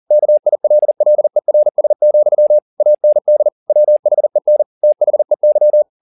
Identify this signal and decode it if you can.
Morse code, heard as a keyed tone, about 40 wpm, 600 Hz. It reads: KILLERS, AND WHEN THEY
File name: the old man and the sea 2 40wpm